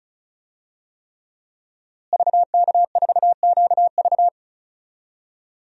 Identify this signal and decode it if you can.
Morse code, heard as a keyed tone, about 34 wpm, 700 Hz. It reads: VK4QV
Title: DXCC-100 Call Signs - 5 Characters - Repeated 3x - 35wpm